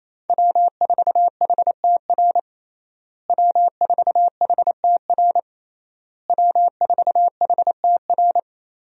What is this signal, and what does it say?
Morse code, heard as a keyed tone, about 28 wpm, 700 Hz. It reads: W4HTR W4HTR W4HTR
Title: USA Call Signs - 5 Characters - Repeated 3x - 28wpm